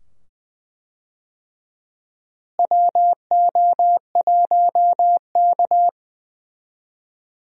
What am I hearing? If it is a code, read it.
Morse code, heard as a keyed tone, about 20 wpm, 700 Hz. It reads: WO1K
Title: USA Call Signs - 4 Characters - Repeated 2x - 20wpm